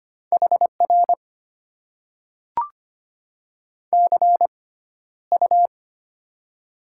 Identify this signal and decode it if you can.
Morse code, heard as a keyed tone, about 25 wpm, spaced as for 13 wpm, 690 Hz. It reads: HR CU
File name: Top 90 QSO Elements - 6x Character Spacing - 25wpm